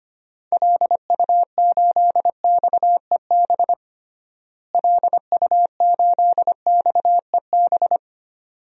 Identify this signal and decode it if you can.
Morse code, heard as a keyed tone, about 25 wpm, 700 Hz. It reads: LU8XEB LU8XEB
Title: DXCC-100 Call Signs - 6 Characters - Repeated 2x - 25wpm